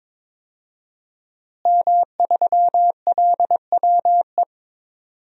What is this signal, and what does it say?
Morse code, heard as a keyed tone, about 22 wpm, 700 Hz. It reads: M3LWE
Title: DXCC-100 Call Signs - 5 Characters - Repeated 2x - 22wpm